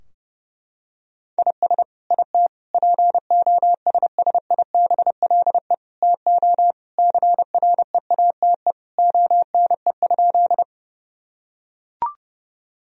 Morse code, heard as a keyed tone, about 30 wpm, 700 Hz. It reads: IS IT POSSIBLE TO CREATE ONE?
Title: Sentences from Top 1000 Words - Review Part 3 - 30wpm